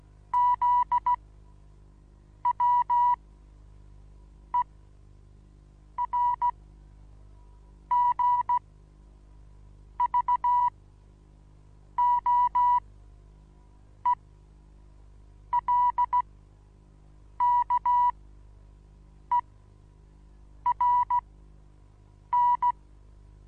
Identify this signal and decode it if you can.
Morse code, heard as a keyed tone, about 16 wpm, spaced as for 6 wpm, 980 Hz. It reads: ZWERGVOELKERN
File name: Onlinelehrgang / morsen / DJ4UF Morsen25